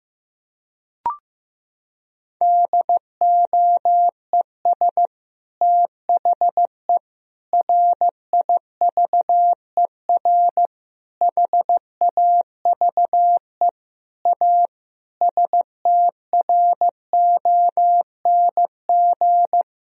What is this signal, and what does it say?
Morse code, heard as a keyed tone, about 15 wpm, 700 Hz. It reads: DOES THE RIVER HAVE A STRONG
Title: Sentences from Top 1000 Words - Review Part 4 - 15wpm